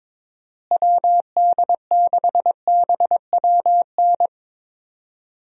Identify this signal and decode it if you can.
Morse code, heard as a keyed tone, about 22 wpm, 700 Hz. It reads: WD6BWN
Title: USA Call Signs - 6 Characters - Repeated 2x - 22wpm